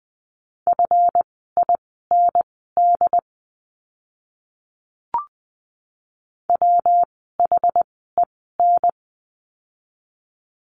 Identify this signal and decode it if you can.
Morse code, heard as a keyed tone, about 20 wpm, spaced as for 14 wpm, 700 Hz. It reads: FIND WHEN
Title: 192 Lesson 31f - Letter v - Words 20wpm